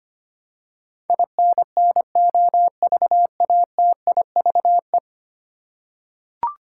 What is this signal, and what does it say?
Morse code, heard as a keyed tone, about 25 wpm, 700 Hz. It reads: INNOVATIVE
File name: Ten Letter Words - 25wpm